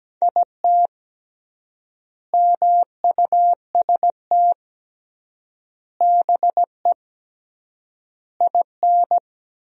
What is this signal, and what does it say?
Morse code, heard as a keyed tone, about 17 wpm, 700 Hz. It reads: IT MUST BE IN
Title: 3x Word-Spacing - Sentences from Top 300 Words - 17wpm